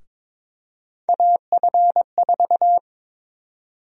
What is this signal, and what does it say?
Morse code, heard as a keyed tone, about 22 wpm, 700 Hz. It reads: AF4
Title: USA Call Sign Prefixes - Repeated 2x - 22wpm